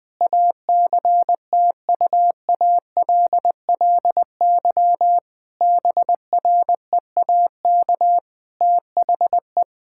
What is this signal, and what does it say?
Morse code, heard as a keyed tone, about 20 wpm, 700 Hz. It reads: ACTUALLY BREAK THE
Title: Sentences from Top 1000 Words - Review Part 6 - 20wpm